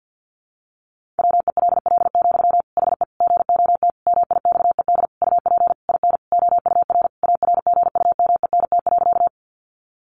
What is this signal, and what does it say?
Morse code, heard as a keyed tone, about 50 wpm, 700 Hz. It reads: WELL, HE GOT MIXED UP IN OUR ARGUMENT.
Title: Sentences from Top 1500 Words 50wpm